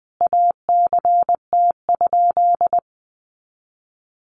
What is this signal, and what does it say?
Morse code, heard as a keyed tone, about 19 wpm, 700 Hz. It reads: ACT?
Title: Sentences from Top 1400 Words 20wpm